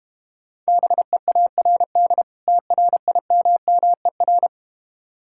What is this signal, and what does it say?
Morse code, heard as a keyed tone, about 32 wpm, 700 Hz. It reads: BEARD TRIMMER